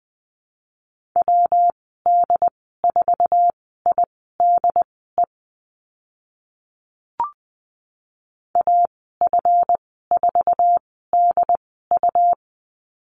Morse code, heard as a keyed tone, about 20 wpm, spaced as for 14 wpm, 700 Hz. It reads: WD4IDE AF4DU
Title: 154 Lesson 25g - Letter p - Call Signs 20wpm